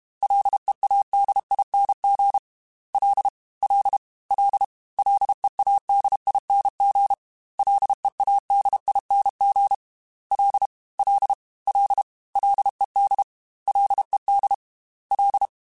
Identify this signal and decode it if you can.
Morse code, heard as a keyed tone, about 32 wpm, 800 Hz. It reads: LEADING L L L LEADING LEADING L L L LED LED L
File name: Ls / Single L m3s / Single L Prefixe 3x2 Twice @30